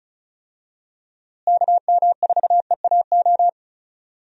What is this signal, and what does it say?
Morse code, heard as a keyed tone, about 36 wpm, 700 Hz. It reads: KM4EAO
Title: USA Call Signs - 6 Characters - Repeated 3x - 35wpm